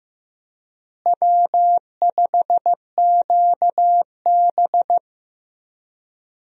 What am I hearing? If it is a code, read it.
Morse code, heard as a keyed tone, about 16 wpm, 700 Hz. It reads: W5QB
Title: USA Call Signs - 4 Characters - Repeated 3x - 15wpm